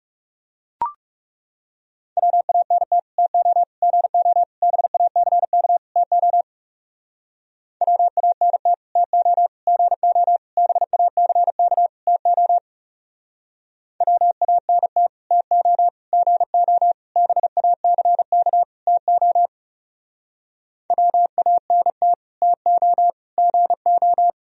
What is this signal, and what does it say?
Morse code, heard as a keyed tone, about 39 wpm, 700 Hz. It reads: WANT TO GO BACK TO WANT TO GO BACK TO WANT TO GO BACK TO WANT TO GO